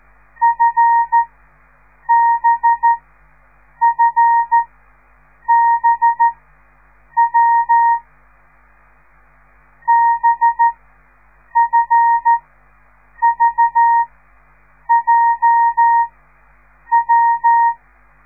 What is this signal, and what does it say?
Morse code, heard as a keyed tone, about 13 wpm, spaced as for 7 wpm, 930 Hz. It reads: FBFBW BFVJW